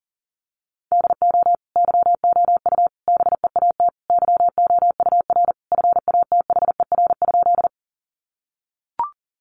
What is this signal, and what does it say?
Morse code, heard as a keyed tone, about 37 wpm, 700 Hz. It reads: DO YOU BEAT YOUR FATHER?
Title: Sentences from Top 1300 Words 40wpm